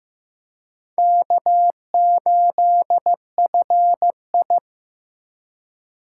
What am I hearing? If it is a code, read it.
Morse code, heard as a keyed tone, about 18 wpm, 700 Hz. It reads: K8FI
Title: Speed Racing - USA 1x2 Call Signs 15wpm